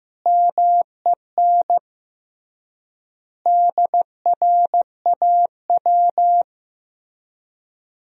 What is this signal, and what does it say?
Morse code, heard as a keyed tone, about 15 wpm, 700 Hz. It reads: MEN DRAW
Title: 3x Word-Spacing - Sentences from Top 300 Words - 15wpm